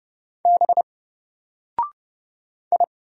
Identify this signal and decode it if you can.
Morse code, heard as a keyed tone, about 31 wpm, 700 Hz. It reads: B I
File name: Single-Letter-Number-30wpm